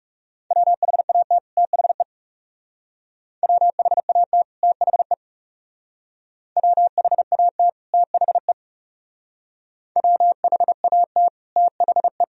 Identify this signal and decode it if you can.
Morse code, heard as a keyed tone, about 38 wpm, 700 Hz. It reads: WHAT THE WHAT THE WHAT THE WHAT THE